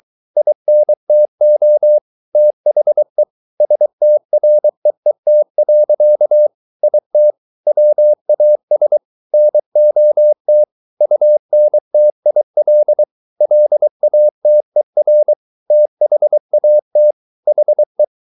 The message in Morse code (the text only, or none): INTO THE STREET. IT WAS NOT UNTIL LATER THAT HE